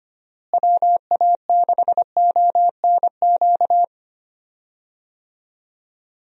Morse code, heard as a keyed tone, about 25 wpm, 700 Hz. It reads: WA6ONQ
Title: USA Call Signs - 6 Characters - Repeated 2x - 25wpm